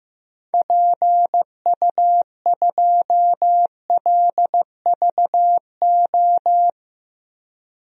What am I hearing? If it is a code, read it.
Morse code, heard as a keyed tone, about 15 wpm, 700 Hz. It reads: PU2LVO
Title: DXCC-100 Call Signs - 6 Characters - Repeated 3x - 15wpm